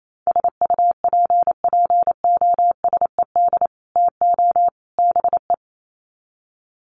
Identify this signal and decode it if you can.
Morse code, heard as a keyed tone, about 28 wpm, 700 Hz. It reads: SUPPOSED TO BE
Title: Sets of 3 Words 28wpm